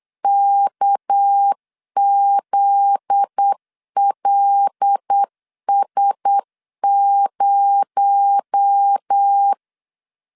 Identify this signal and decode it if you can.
Morse code, heard as a keyed tone, about 8 wpm, 800 Hz. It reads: KZLS0
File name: CW / DET / 1 9 1500C S C 8wpm 5653